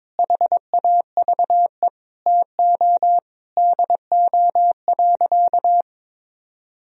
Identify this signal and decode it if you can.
Morse code, heard as a keyed tone, about 22 wpm, 700 Hz. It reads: HAVE TO DO.
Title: Sentences from Top 1000 Words - Review Part 2 - 22wpm